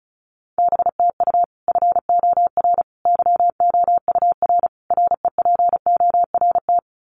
Morse code, heard as a keyed tone, about 34 wpm, 700 Hz. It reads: BTU FOR YOUR REPORT
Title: Common QSO Phrases 35wpm